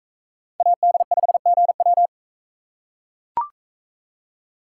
Morse code, heard as a keyed tone, about 40 wpm, 700 Hz. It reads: AD5GW